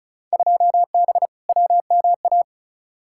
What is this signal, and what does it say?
Morse code, heard as a keyed tone, about 34 wpm, 700 Hz. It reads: 2B WMA
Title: Field Day Exchanges Encore - 35wpm